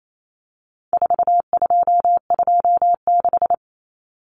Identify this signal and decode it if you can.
Morse code, heard as a keyed tone, about 28 wpm, 700 Hz. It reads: 4226
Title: Common Cut Numbers - 4-Digit 28wpm